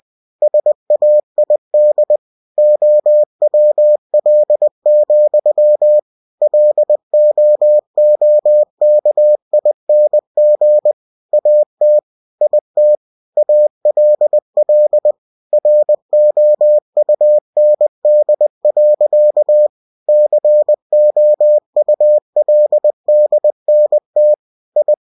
Morse code, heard as a keyed tone, about 20 wpm, 600 Hz. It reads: SAID OWL, LOOKING AT IT ALL ROUND. COULDNT I
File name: cw-books / winnie-the-pooh / winnie-20wpm-page-0051